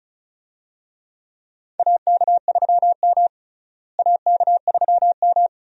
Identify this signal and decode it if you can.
Morse code, heard as a keyed tone, about 36 wpm, 700 Hz. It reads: AK3M AK3M
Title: USA Call Signs - 4 Characters - Repeated 2x - 35wpm